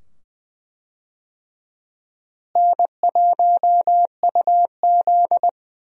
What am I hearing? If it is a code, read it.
Morse code, heard as a keyed tone, about 20 wpm, 700 Hz. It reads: N1UZ
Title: USA Call Signs - 4 Characters - Repeated 2x - 20wpm